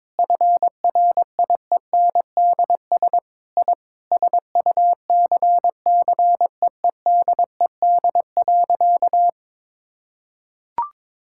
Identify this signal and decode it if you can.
Morse code, heard as a keyed tone, about 22 wpm, 700 Hz. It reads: FRIENDS I SUCCEEDED.
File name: Sentences with 7-Word Phrases - 22wpm